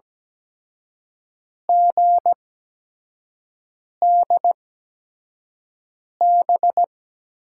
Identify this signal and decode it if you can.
Morse code, heard as a keyed tone, about 17 wpm, spaced as for 5 wpm, 700 Hz. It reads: GDB
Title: USA Call Signs - 6 Characters - 8x Character Spacing - 17wpm